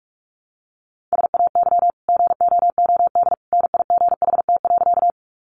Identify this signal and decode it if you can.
Morse code, heard as a keyed tone, about 45 wpm, 700 Hz. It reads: SAY GOOD NIGHT.
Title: Sentences from Top 300 Words 45wpm